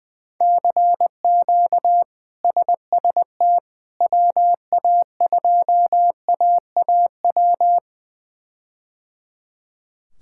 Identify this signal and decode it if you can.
Morse code, heard as a keyed tone, about 21 wpm, 700 Hz. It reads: CQ SST WA2AAW
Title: K1USN SST QSO Practice - 20wpm